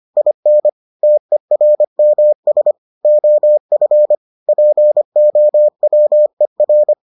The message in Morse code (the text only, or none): IN TERMS OF POWER